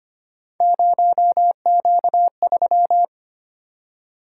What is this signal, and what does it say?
Morse code, heard as a keyed tone, about 25 wpm, 700 Hz. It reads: 0Q3
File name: Three Letter-Number Combinations - 25wpm